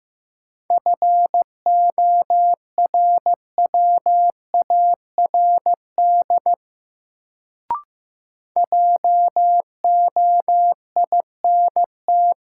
Morse code, heard as a keyed tone, about 15 wpm, 700 Hz. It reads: FORWARD JOINT
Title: Sentences from Top 1300 Words - New Word Drill Alt - 15wpm